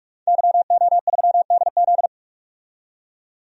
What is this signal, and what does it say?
Morse code, heard as a keyed tone, about 44 wpm, 700 Hz. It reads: YO3DZ
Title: DXCC-100 Call Signs - 5 Characters - 45wpm